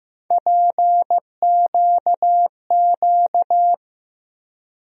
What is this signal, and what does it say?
Morse code, heard as a keyed tone, about 15 wpm, 700 Hz. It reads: PQQ
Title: USA Call Sign Postfixes - 3 Letters 15wpm